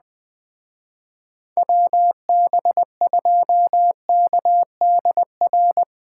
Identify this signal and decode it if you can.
Morse code, heard as a keyed tone, about 21 wpm, 700 Hz. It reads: WB2KDR
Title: Speed Racing - USA 2x3 Call Signs 17wpm